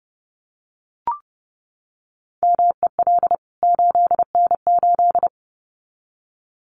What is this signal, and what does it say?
Morse code, heard as a keyed tone, about 30 wpm, 700 Hz. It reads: MEL 8N8
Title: CWT Exchange 30wpm